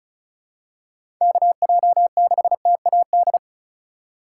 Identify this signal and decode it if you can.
Morse code, heard as a keyed tone, about 34 wpm, 700 Hz. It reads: KJ6TAD
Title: USA Call Signs - 4x Character Spacing - 35wpm